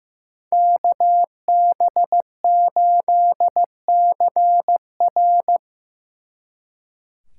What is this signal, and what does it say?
Morse code, heard as a keyed tone, about 18 wpm, 700 Hz. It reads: KB8CR
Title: Speed Racing - USA 2x2 Call Signs 15wpm